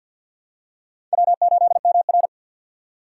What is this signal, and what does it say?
Morse code, heard as a keyed tone, about 52 wpm, 700 Hz. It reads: W8MR
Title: USA Call Signs - 4 Characters - 3x Character Spacing - 50wpm